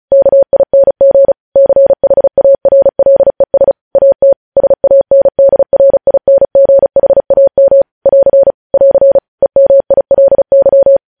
Morse code, heard as a keyed tone, about 36 wpm, 550 Hz. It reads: KING CHARLES AT SANDRINGHAM <AR> <AR> EMILY
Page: News Headlines in Morse Code at 35 WPM